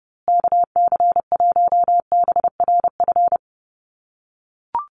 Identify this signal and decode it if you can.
Morse code, heard as a keyed tone, about 30 wpm, 700 Hz. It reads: KC1BRF
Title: USA Call Signs Repeated 2x 30wpm